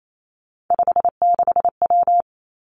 Morse code, heard as a keyed tone, about 28 wpm, 700 Hz. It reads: 56W
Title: Three Challenging Letter-Number Combinations 28wpm